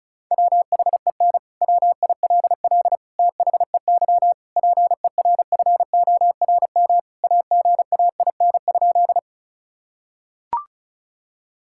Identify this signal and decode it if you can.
Morse code, heard as a keyed tone, about 34 wpm, 700 Hz. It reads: WHEN WILL THEY PERFORM AGAIN?
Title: Sentences from Top 1000 Words - Review Part 5 - 35wpm